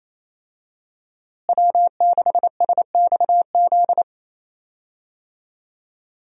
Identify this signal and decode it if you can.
Morse code, heard as a keyed tone, about 28 wpm, 700 Hz. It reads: W6SXZ
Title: USA Call Signs - 5 Characters - Repeated 3x - 28wpm